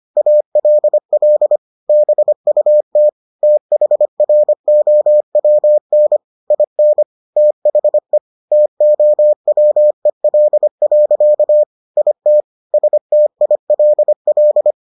Morse code, heard as a keyed tone, about 25 wpm, 600 Hz. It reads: ALL BUT THROWN IN THE TOWEL. IT STILL